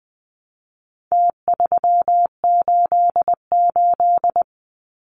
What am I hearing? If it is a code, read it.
Morse code, heard as a keyed tone, about 20 wpm, 700 Hz. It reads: T388
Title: Numbers - Contest Style - 4-Digit 20wpm